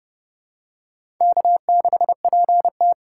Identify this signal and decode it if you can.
Morse code, heard as a keyed tone, about 30 wpm, 700 Hz. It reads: K6PT
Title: USA Call Signs - 4 Characters - 3x Character Spacing - 30wpm